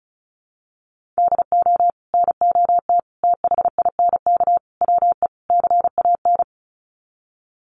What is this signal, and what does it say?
Morse code, heard as a keyed tone, about 34 wpm, 700 Hz. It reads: DO NOT THINK WE CAN
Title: Sets of 5 Words Encore2 35wpm